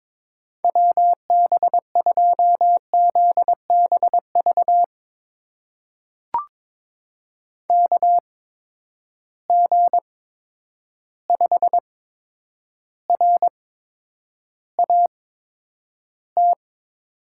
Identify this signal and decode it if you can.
Morse code, heard as a keyed tone, about 22 wpm, spaced as for 9 wpm, 700 Hz. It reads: WB2ZBV KG5RAT
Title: USA Call Signs - 6 Characters - 8x Character Spacing - 22wpm